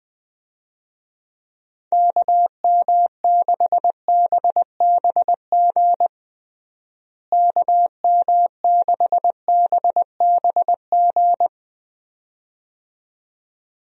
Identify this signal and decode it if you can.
Morse code, heard as a keyed tone, about 20 wpm, 700 Hz. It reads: KM6BBG KM6BBG
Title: USA Call Signs - 6 Characters - Repeated 2x - 20wpm